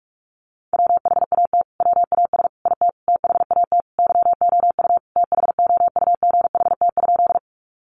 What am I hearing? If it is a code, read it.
Morse code, heard as a keyed tone, about 44 wpm, 700 Hz. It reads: WHAT WAS IT THAT YOU THOUGHT?
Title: Sentences from Top 300 Words 45wpm